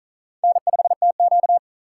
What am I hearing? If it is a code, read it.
Morse code, heard as a keyed tone, about 47 wpm, 700 Hz. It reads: N5TQ